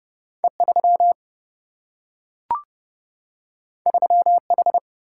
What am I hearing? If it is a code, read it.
Morse code, heard as a keyed tone, about 29 wpm, 700 Hz. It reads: E3 3H
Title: Single-Letter Single-Number 30wpm